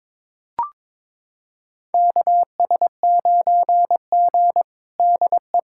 Morse code, heard as a keyed tone, about 21 wpm, 700 Hz. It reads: KS9G DE
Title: USA Callsign DE Callsign - 4 Characters - 22wpm